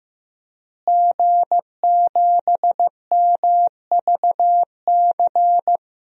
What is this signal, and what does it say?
Morse code, heard as a keyed tone, about 15 wpm, 700 Hz. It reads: G7MVC
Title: DXCC-100 Call Signs - 5 Characters - Repeated 3x - 15wpm